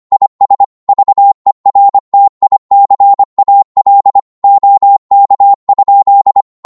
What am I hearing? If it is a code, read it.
Morse code, heard as a keyed tone, about 25 wpm, 800 Hz. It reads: IS VERTICAL OK?
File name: qso-25wpm